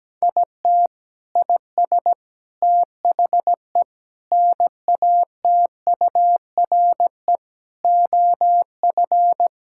Morse code, heard as a keyed tone, about 17 wpm, 700 Hz. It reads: IT IS THE NATURE OF
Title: Sentences from Top 1000 Words - Review Part 5 - 17wpm